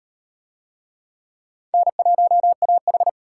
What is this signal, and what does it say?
Morse code, heard as a keyed tone, about 38 wpm, 700 Hz. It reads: N1AH